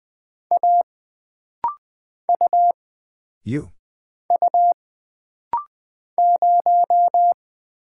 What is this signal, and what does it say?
Morse code, heard as a keyed tone, about 20 wpm, 700 Hz. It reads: A U U 0
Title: Single Character (Letter:Number:Punctuation:Prosign) - Rapid-Fire - 20wpm